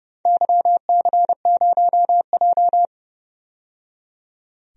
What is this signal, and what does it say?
Morse code, heard as a keyed tone, about 32 wpm, 700 Hz. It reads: YC0J
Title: DXCC-100 Call Signs - 4 Characters - 30wpm